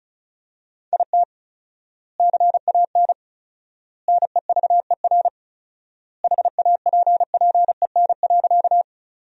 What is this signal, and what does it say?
Morse code, heard as a keyed tone, about 36 wpm, 700 Hz. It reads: IT CAN NEVER HAPPEN.
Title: 4x Word-Spacing - Sentences from Top 400 Words - 35wpm